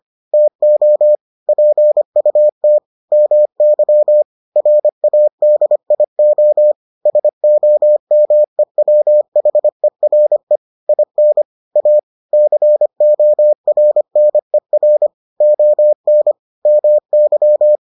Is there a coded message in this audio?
Yes: Morse code from 0.3 to 17.8 s, heard as a keyed tone, about 25 wpm, 600 Hz. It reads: TO PUT MY RADIO SOMEWHERE IN A CORNER ON MY